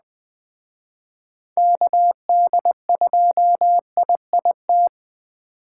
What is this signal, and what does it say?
Morse code, heard as a keyed tone, about 21 wpm, 700 Hz. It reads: KD2IIT